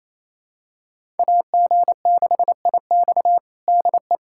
Morse code, heard as a keyed tone, about 28 wpm, 700 Hz. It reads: AG6IX DE
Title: USA Callsign DE Callsign - 5 Characters - 28wpm